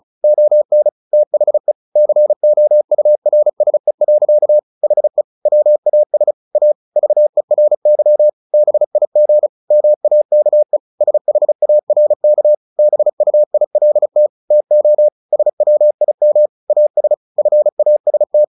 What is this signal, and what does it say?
Morse code, heard as a keyed tone, about 34 wpm, 600 Hz. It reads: ON THE COURSE. HE WAS A VERY BIG MAKE SHARK BUILT TO SWIM AS FAST
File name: the old man and the sea 2 35wpm